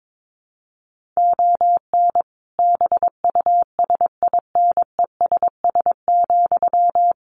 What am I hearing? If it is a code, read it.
Morse code, heard as a keyed tone, about 22 wpm, 700 Hz. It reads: ON BUSINESS,
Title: Sentences from Top 1300 Words 22wpm